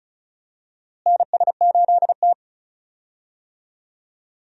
Morse code, heard as a keyed tone, about 36 wpm, 700 Hz. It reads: NS8T
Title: USA Call Signs - 4 Characters - Repeated 2x - 35wpm